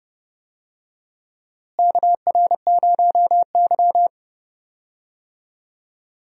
Morse code, heard as a keyed tone, about 31 wpm, 700 Hz. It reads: KR0Y
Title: USA Call Signs - 4 Characters - Repeated 2x - 30wpm